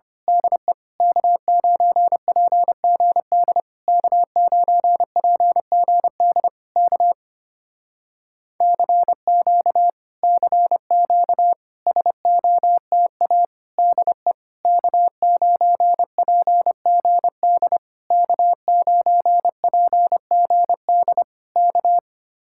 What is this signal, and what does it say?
Morse code, heard as a keyed tone, about 24 wpm, 700 Hz. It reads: DE K9PGD K9PGD K CQ CQ SOTA DE K9PGD K9PGD K
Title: Speed-Racing Common QSO Phrases - 25wpm